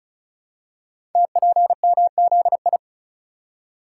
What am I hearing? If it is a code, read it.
Morse code, heard as a keyed tone, about 36 wpm, 690 Hz. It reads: TPMZI
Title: Code Groups - 5 Letters - 35wpm